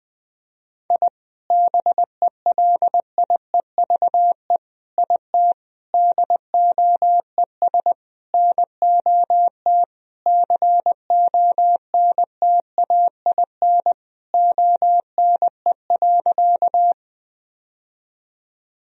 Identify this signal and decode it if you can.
Morse code, heard as a keyed tone, about 19 wpm, 700 Hz. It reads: I BELIEVE IT DOES NOT CONTAIN ONE.
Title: Sentences from Top 1000 Words - Review Part 2 - 20wpm